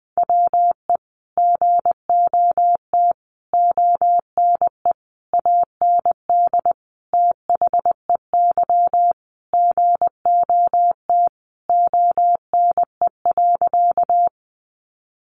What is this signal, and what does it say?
Morse code, heard as a keyed tone, about 20 wpm, 700 Hz. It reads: WE GOT ONE AND THEY GOT ONE.
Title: Sentences from Top 300 Words 20wpm